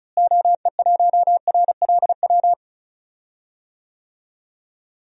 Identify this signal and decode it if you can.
Morse code, heard as a keyed tone, about 36 wpm, 700 Hz. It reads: OE1RLW
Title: DXCC-100 Call Signs - 6 Characters - Repeated 3x - 35wpm